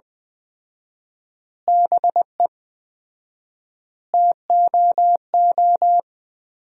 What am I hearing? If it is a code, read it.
Morse code, heard as a keyed tone, about 20 wpm, 700 Hz. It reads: BE TOO
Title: 4x Word-Spacing - Sentences from Top 300 Words - 20wpm